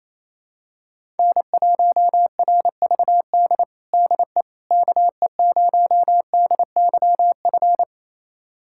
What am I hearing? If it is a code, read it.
Morse code, heard as a keyed tone, about 28 wpm, 700 Hz. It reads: N1RVD DE KE0DYF
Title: USA Callsign DE Callsign 28wpm